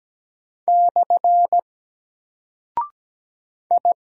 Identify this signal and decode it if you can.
Morse code, heard as a keyed tone, about 17 wpm, 700 Hz. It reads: / I
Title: Single Character (Letter:Number:Punctuation:Prosign) Alt 17wpm